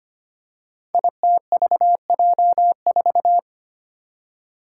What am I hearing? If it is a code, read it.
Morse code, heard as a keyed tone, about 25 wpm, 700 Hz. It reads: ITVJ4
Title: Five Challenging Letter-Number Combinations 25wpm